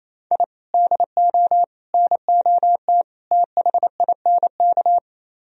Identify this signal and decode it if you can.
Morse code, heard as a keyed tone, about 36 wpm, 700 Hz. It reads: I DO NOT THINK